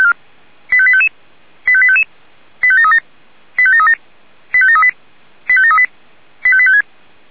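Five-tone selective call (ZVEI), heard as quick runs of five tones, tones of about 70 ms. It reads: 85855 85855 85737 85737 85738 85738 85856
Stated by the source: ZVEI five-tone selective call 85853 received by SDR.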